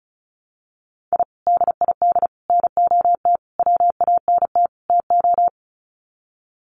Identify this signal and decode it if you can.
Morse code, heard as a keyed tone, about 34 wpm, 700 Hz. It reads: I DID NOT WANT TO
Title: Sets of 5 Words Encore 35wpm